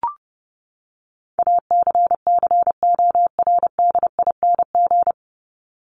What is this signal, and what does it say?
Morse code, heard as a keyed tone, about 30 wpm, 700 Hz. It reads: ACCORDING
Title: LY Suffix - Top 100 Words with Base Words 30wpm